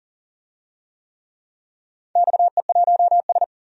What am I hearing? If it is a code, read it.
Morse code, heard as a keyed tone, about 42 wpm, 700 Hz. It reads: XE1S
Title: DXCC-100 Call Signs - 4 Characters - 40wpm